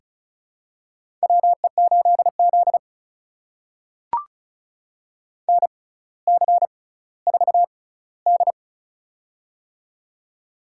Morse code, heard as a keyed tone, about 36 wpm, spaced as for 18 wpm, 700 Hz. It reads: WE8Z NC4D
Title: USA Call Signs - 6x Character Spacing - 35wpm